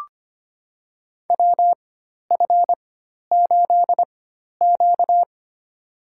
Morse code, heard as a keyed tone, about 25 wpm, spaced as for 12 wpm, 700 Hz. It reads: WF8Q
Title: USA Call Signs - 4 Characters - 4x Character Spacing - 25wpm